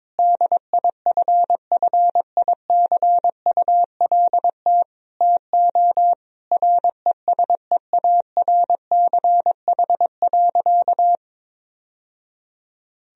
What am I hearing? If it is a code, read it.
Morse code, heard as a keyed tone, about 22 wpm, 700 Hz. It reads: DIFFICULT TO RESEARCH.
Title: Sentences from Top 1000 Words - Review Part 4 - 22wpm